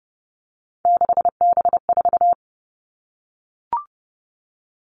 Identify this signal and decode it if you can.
Morse code, heard as a keyed tone, about 30 wpm, 700 Hz. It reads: 6B4
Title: All Cut Numbers - 3-Digit 30wpm